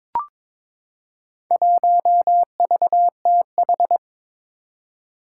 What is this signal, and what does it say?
Morse code, heard as a keyed tone, about 22 wpm, 700 Hz. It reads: 1VTH
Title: Four Challenging Letter-Number Combinations 22wpm